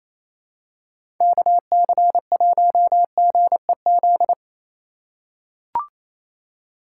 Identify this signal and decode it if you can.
Morse code, heard as a keyed tone, about 28 wpm, 700 Hz. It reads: KC1GEZ
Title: USA Call Signs - 6 Characters - 3x Character Spacing - 28wpm